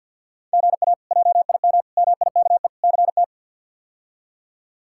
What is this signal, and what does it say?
Morse code, heard as a keyed tone, about 49 wpm, 700 Hz. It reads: GA JIM MIKE CT
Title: K1USN SST QSO Practice - 50wpm